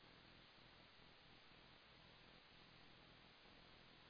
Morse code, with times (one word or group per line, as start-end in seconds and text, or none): none